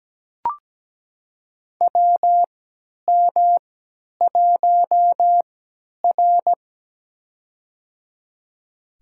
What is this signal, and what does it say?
Morse code, heard as a keyed tone, about 17 wpm, spaced as for 10 wpm, 700 Hz. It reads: WM1R
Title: USA Call Signs - 4 Characters - 3x Character Spacing - 17wpm